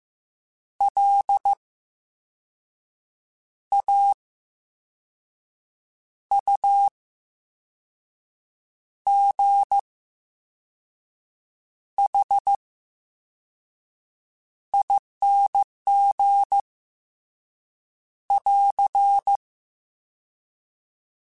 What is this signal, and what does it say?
Morse code, heard as a keyed tone, about 15 wpm, spaced as for 5 wpm, 800 Hz. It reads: LAUGHING <AR>
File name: Special INGs / Special INGs-mp3s / Slow ing @13